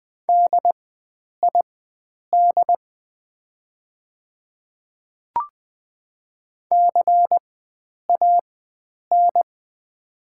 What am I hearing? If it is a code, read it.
Morse code, heard as a keyed tone, about 20 wpm, spaced as for 9 wpm, 700 Hz. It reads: DID CAN
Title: 4x Character Spacing - Top 300 Words - 20wpm